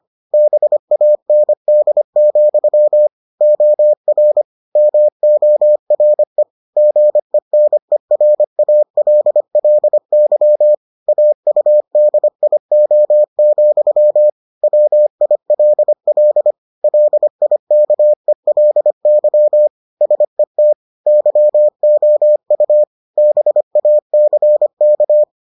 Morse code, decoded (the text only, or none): BAND, OR MORE GENERALLY AUDIO, WILL LIKELY SET YOU BACK